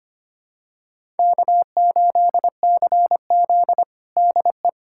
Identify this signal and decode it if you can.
Morse code, heard as a keyed tone, about 25 wpm, 700 Hz. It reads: K8CZ DE
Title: USA Callsign DE Callsign - 4 Characters - 25wpm